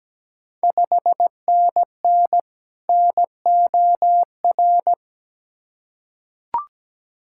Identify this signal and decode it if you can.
Morse code, heard as a keyed tone, about 17 wpm, 700 Hz. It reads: 5NN NOR
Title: Ontario QSO Party Exchange 17wpm